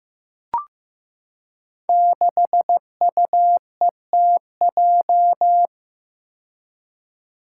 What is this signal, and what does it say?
Morse code, heard as a keyed tone, about 15 wpm, 700 Hz. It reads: 6UETJ
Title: Five Challenging Letter-Number Combinations 15wpm